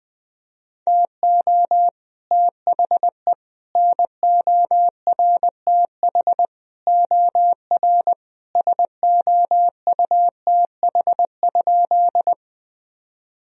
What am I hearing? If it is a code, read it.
Morse code, heard as a keyed tone, about 21 wpm, 700 Hz. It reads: TO THE NORTH OR SOUTH?
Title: Sentences from Top 1000 Words - Review Part 2 - 20wpm